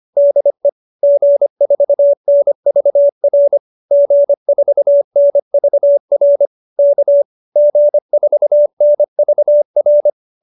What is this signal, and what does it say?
Morse code, heard as a keyed tone, about 25 wpm, 590 Hz. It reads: DE G4NVR G4NVR K G4NVR
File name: QSO315 25